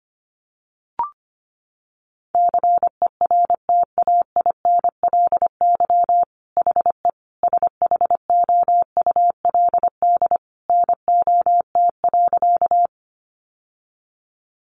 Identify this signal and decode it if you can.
Morse code, heard as a keyed tone, about 25 wpm, 700 Hz. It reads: CERTAINLY HE SHOULD NOT.
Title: Sentences from Top 900 Words 25wpm